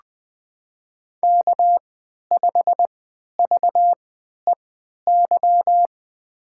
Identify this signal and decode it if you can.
Morse code, heard as a keyed tone, about 20 wpm, spaced as for 11 wpm, 700 Hz. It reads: K5VEY
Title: USA Call Signs - 5 Characters - 3x Character Spacing - 20wpm